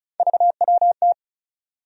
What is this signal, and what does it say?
Morse code, heard as a keyed tone, about 34 wpm, 700 Hz. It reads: VWT
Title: USA Call Sign Postfixes - 3 Letters 35wpm